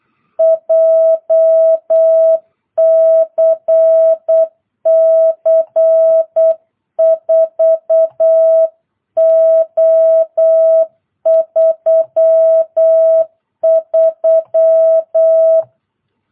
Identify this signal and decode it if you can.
Morse code, heard as a keyed tone, about 8 wpm, 660 Hz. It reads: JCC 4O33